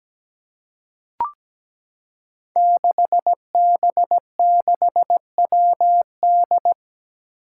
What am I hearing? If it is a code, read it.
Morse code, heard as a keyed tone, about 17 wpm, 700 Hz. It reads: 6B6WD
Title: Five Challenging Letter-Number Combinations 17wpm